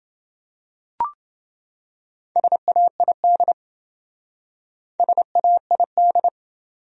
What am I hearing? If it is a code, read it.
Morse code, heard as a keyed tone, about 27 wpm, 700 Hz. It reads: SAID SAID